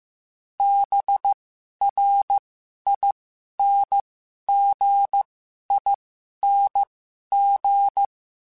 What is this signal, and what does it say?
Morse code, heard as a keyed tone, about 15 wpm, spaced as for 11 wpm, 800 Hz. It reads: BRINGING
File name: Special INGs / Special INGs-mp3s / Dble ings @10